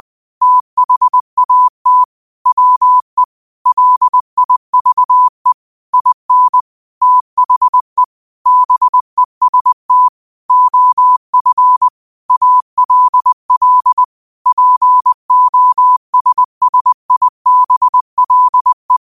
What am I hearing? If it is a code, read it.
Morse code, heard as a keyed tone, about 20 wpm, 1000 Hz. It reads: THAT WE LIVE IN THE BEST OF ALL POSSIBLE
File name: qotdfiles / 20wpm